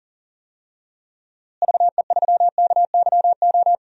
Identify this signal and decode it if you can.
Morse code, heard as a keyed tone, about 40 wpm, 700 Hz. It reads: VE3KYO
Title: DXCC-100 Call Signs - 6 Characters - Repeated 2x - 40wpm